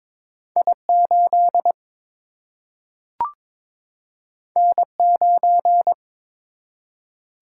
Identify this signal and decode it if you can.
Morse code, heard as a keyed tone, about 21 wpm, 700 Hz. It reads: I 8 N 9
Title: Single-Letter Single-Number 22wpm